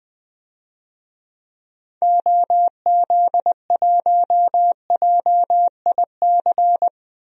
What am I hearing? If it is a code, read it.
Morse code, heard as a keyed tone, about 20 wpm, 700 Hz. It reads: OZ1JIC
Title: DXCC-100 Call Signs - 6 Characters - 20wpm